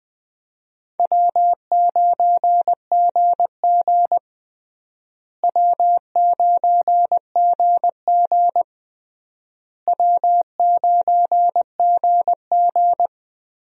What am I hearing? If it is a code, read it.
Morse code, heard as a keyed tone, about 20 wpm, 700 Hz. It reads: W9GG W9GG W9GG
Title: USA Call Signs - 4 Characters - Repeated 3x - 20wpm